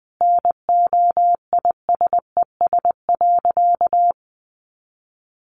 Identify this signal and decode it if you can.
Morse code, heard as a keyed tone, about 20 wpm, 700 Hz. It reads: NOISES.
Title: Additional Easy Phrases 20wpm